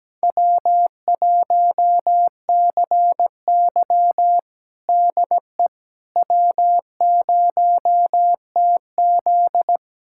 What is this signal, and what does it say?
Morse code, heard as a keyed tone, about 17 wpm, 700 Hz. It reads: W1CY DE W0TZ
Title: USA Callsign DE Callsign - 4 Characters - 17wpm